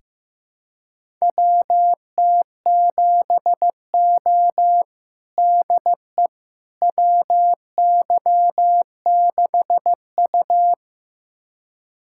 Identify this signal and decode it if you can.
Morse code, heard as a keyed tone, about 15 wpm, 700 Hz. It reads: WT7O DE WY6U
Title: USA Callsign DE Callsign - 4 Characters - 15wpm